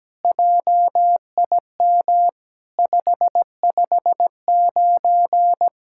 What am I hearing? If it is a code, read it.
Morse code, heard as a keyed tone, about 17 wpm, 700 Hz. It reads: JIM 559
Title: SOTA QSO Practice 17wpm